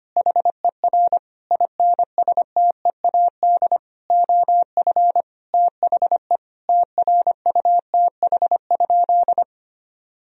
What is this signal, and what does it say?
Morse code, heard as a keyed tone, about 25 wpm, 700 Hz. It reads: HER INSTEAD OF THE TRUTH?
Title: Sentences from Top 1000 Words - Review Part 4 - 25wpm